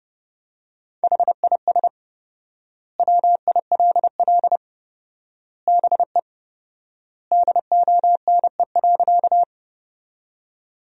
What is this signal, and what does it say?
Morse code, heard as a keyed tone, about 30 wpm, 700 Hz. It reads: HIS WILL BE DONE.
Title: 4x Word-Spacing - Sentences from Top 500 Words - 30wpm